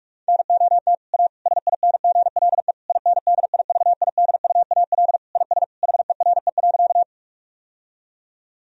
Morse code, heard as a keyed tone, about 47 wpm, 700 Hz. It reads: NOT A SINGLE INDIVIDUAL IS HERE.
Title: Sentences from Top 1000 Words - Review Part 4 - 45wpm